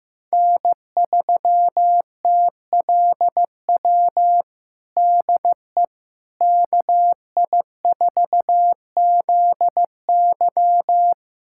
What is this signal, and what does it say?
Morse code, heard as a keyed tone, about 15 wpm, 700 Hz. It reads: N3TLW DE KI4ZY
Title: USA Callsign DE Callsign - 5 Characters - 15wpm